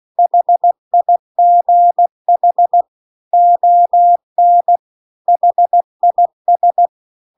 Morse code, heard as a keyed tone, about 16 wpm, 700 Hz. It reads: HIGH ON HIS